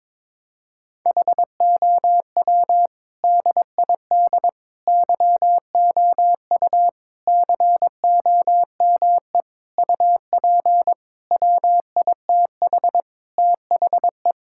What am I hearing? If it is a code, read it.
Morse code, heard as a keyed tone, about 22 wpm, 700 Hz. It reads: HOW DID YOU COME UP WITH THE
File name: practice / 22wpm / Sets of 8 Words - 22wpm